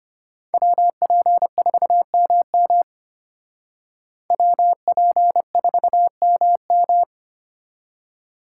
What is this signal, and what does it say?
Morse code, heard as a keyed tone, about 32 wpm, 700 Hz. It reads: WP4MM WP4MM